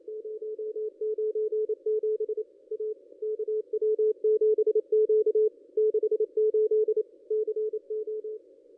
Morse code, heard as a keyed tone, about 28 wpm, 430 Hz. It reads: 097 A KW7Q 68 CO
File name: contests / ss-cw-k5tr-op-n5rz -1900